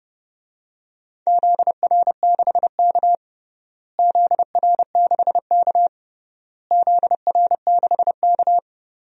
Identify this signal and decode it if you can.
Morse code, heard as a keyed tone, about 32 wpm, 700 Hz. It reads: ZR6K ZR6K ZR6K
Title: DXCC-100 Call Signs - 4 Characters - Repeated 3x - 30wpm